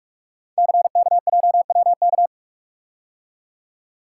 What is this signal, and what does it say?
Morse code, heard as a keyed tone, about 44 wpm, 700 Hz. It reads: CKJWK
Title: Code Groups - 5 Letters - 45wpm